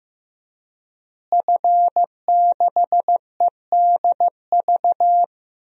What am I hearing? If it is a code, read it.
Morse code, heard as a keyed tone, about 15 wpm, 700 Hz. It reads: F6EDV
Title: DXCC-100 Call Signs - 5 Characters - Repeated 2x - 15wpm